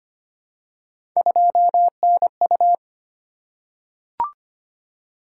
Morse code, heard as a keyed tone, about 25 wpm, 700 Hz. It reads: 2NU
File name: Three Letter-Number Combinations - 25wpm